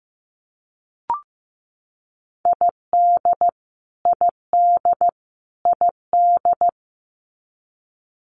Morse code, heard as a keyed tone, about 15 wpm, 700 Hz. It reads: ID ID ID
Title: US States Repeated 3x 15wpm